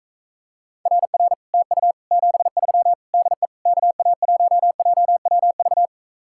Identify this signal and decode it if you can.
Morse code, heard as a keyed tone, about 45 wpm, 700 Hz. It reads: RR TU 73 DE KA1JWV